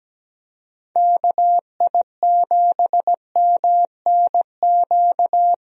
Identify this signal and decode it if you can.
Morse code, heard as a keyed tone, about 17 wpm, 700 Hz. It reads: KI7MNQ
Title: USA Call Signs - 6 Characters - Repeated 2x - 17wpm